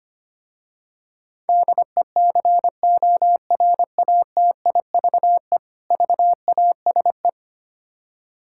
Morse code, heard as a keyed tone, about 25 wpm, 700 Hz. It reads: DECORATIVE VASE
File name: practice / 25wpm / Difficult Words with Hints - 25wpm